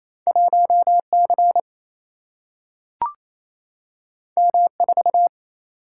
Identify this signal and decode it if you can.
Morse code, heard as a keyed tone, about 28 wpm, 700 Hz. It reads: 1C M4
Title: Single-Letter Single-Number 28wpm